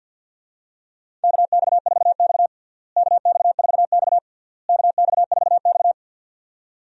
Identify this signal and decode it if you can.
Morse code, heard as a keyed tone, about 47 wpm, 700 Hz. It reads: KX4X KX4X KX4X
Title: USA Call Signs - 4 Characters - Repeated 3x - 50wpm